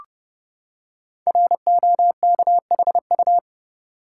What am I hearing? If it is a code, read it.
Morse code, heard as a keyed tone, about 30 wpm, 700 Hz. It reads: ROKHU
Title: Code Groups - 5 Letters - 30wpm